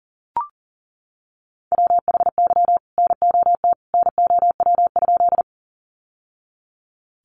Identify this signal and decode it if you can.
Morse code, heard as a keyed tone, about 40 wpm, 700 Hz. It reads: WHY NOT NOW?
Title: Sentences from Top 600 Words 40wpm